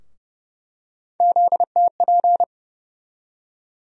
Morse code, heard as a keyed tone, about 30 wpm, 700 Hz. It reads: ZTP
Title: USA Call Sign Postfixes - 3 Letters 30wpm